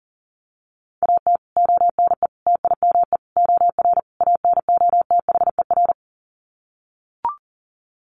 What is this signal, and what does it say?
Morse code, heard as a keyed tone, about 40 wpm, 700 Hz. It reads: AT ONE TIME OR ANOTHER
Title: Sets of 5 Words Encore 40wpm